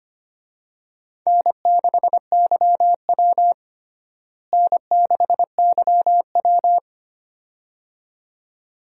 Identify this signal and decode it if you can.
Morse code, heard as a keyed tone, about 25 wpm, 700 Hz. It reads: N6YW N6YW
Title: USA Call Signs - 4 Characters - Repeated 2x - 25wpm